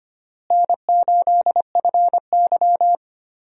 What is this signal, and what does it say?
Morse code, heard as a keyed tone, about 25 wpm, 700 Hz. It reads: N8FY
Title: USA Call Signs - 4 Characters - Repeated 3x - 25wpm